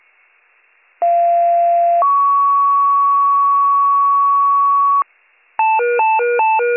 Two-tone page, heard as a one-second tone, then a three-second tone, about 690 then 1120 Hz. It is heard from 1.0 to 5.0 s.